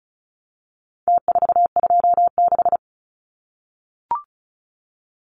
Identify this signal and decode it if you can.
Morse code, heard as a keyed tone, about 34 wpm, 700 Hz. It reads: T426
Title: Numbers - Contest Style - 4-Digit 35wpm